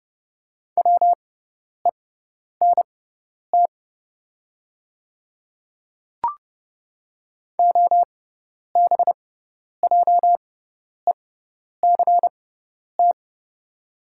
Morse code, heard as a keyed tone, about 30 wpm, spaced as for 10 wpm, 700 Hz. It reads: WENT OBJECT
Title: 6x Character Spacing - Top 500 Words - 30wpm